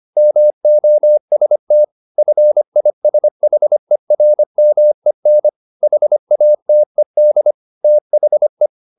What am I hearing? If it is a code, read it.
Morse code, heard as a keyed tone, about 25 wpm, 600 Hz. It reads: MOST FISHERMEN HATED THE
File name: the old man and the sea part 1 25wpm